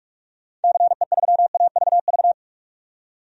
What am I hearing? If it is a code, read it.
Morse code, heard as a keyed tone, about 42 wpm, 700 Hz. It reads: CE3AVV
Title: DXCC-100 Call Signs - 6 Characters - 45wpm